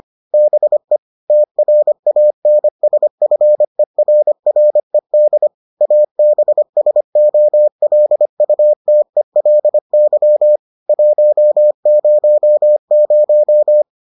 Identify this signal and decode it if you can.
Morse code, heard as a keyed tone, about 25 wpm, 600 Hz. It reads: BE TRANSFERRED ABSOLUTELY 100